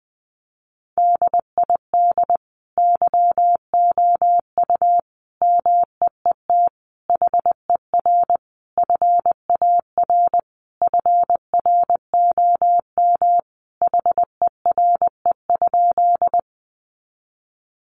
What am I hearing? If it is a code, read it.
Morse code, heard as a keyed tone, about 20 wpm, 700 Hz. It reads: DID YOU MEET HER FAR FROM HERE?
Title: Sentences from Top 1100 Words 20wpm